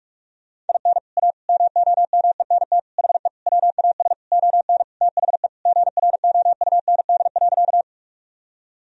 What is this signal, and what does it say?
Morse code, heard as a keyed tone, about 46 wpm, 700 Hz. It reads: IN A MOMENT HE WAS ON THE GROUND.
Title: Sentences from Top 1000 Words - Review Part 2 - 45wpm